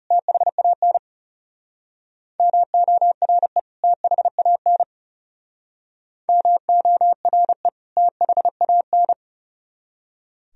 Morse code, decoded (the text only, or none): THAN MORE THAN MORE THAN